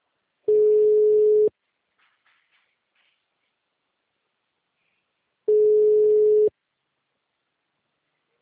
Ringback tone (Europe, 425 Hz).
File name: out- -2029- - - .20